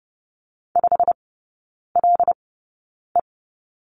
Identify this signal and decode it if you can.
Morse code, heard as a keyed tone, about 30 wpm, spaced as for 9 wpm, 700 Hz. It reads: 5LE
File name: Lesson 17e - Letter u - Three Characters 30wpm